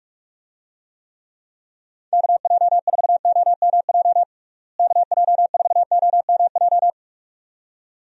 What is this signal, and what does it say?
Morse code, heard as a keyed tone, about 47 wpm, 700 Hz. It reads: KJ4OMJ KJ4OMJ
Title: USA Call Signs - 6 Characters - Repeated 2x - 45wpm